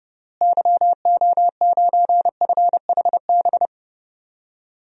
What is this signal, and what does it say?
Morse code, heard as a keyed tone, about 31 wpm, 700 Hz. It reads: YO9FHB
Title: DXCC-100 Call Signs - 6 Characters - Repeated 2x - 30wpm